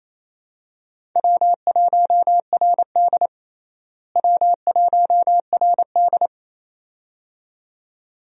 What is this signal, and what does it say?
Morse code, heard as a keyed tone, about 28 wpm, 700 Hz. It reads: W1RD W1RD
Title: USA Call Signs - 4 Characters - Repeated 2x - 28wpm